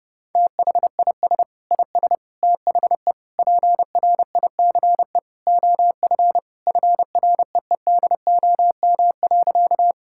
Morse code, heard as a keyed tone, about 30 wpm, 700 Hz. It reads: THIS IS THE PRICE OF FREEDOM.
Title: Sentences from Top 1000 Words - Review Part 5 - 30wpm